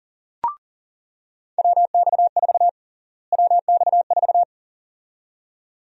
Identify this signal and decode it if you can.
Morse code, heard as a keyed tone, about 40 wpm, 700 Hz. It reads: WX4 WX4
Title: USA Call Sign Prefixes - Repeated 2x - 40wpm